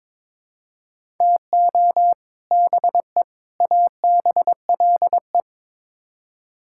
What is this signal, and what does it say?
Morse code, heard as a keyed tone, about 22 wpm, 700 Hz. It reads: TO BE ABLE
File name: practice / 22wpm / Build A Phrase - 5-Word Phrases - 22wpm